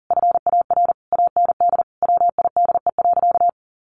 Morse code, heard as a keyed tone, about 40 wpm, 700 Hz. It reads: FAR AND WIDE.
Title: Sentences from Top 1200 Words 40wpm